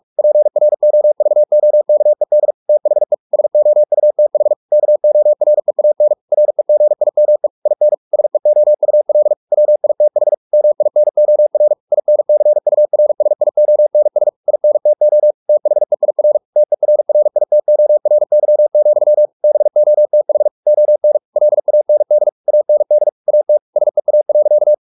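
Morse code, heard as a keyed tone, about 44 wpm, 600 Hz. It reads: PROVOKED THE SOUTH KOREAN REGIME IN SEOUL WITH MINOR INCURSIONS INTO THEIR TERRITORY, BOTH ON LAND AND AT SEA.
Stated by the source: Morse practice at 45 wpm, the novel Never by Ken Follett in English, from CW med Gnister.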